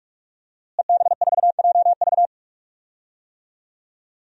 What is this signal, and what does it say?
Morse code, heard as a keyed tone, about 41 wpm, 700 Hz. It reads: EB4JV
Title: DXCC-100 Call Signs - 5 Characters - Repeated 3x - 45wpm